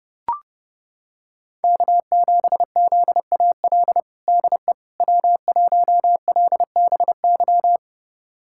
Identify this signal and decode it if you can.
Morse code, heard as a keyed tone, about 30 wpm, 700 Hz. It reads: K7ZAL DE W1LBY
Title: USA Callsign DE Callsign - 5 Characters - 30wpm